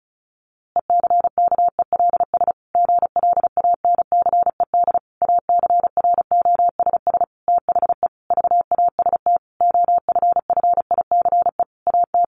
Morse code, heard as a keyed tone, about 34 wpm, 700 Hz. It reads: ECKELS GLANCED ACROSS THE VAST OFFICE AT
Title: A Sound Of Thunder 35wpm